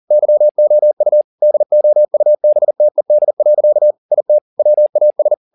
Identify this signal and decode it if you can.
Morse code, heard as a keyed tone, about 40 wpm, 600 Hz. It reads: YOU DOUBTED. IT WAS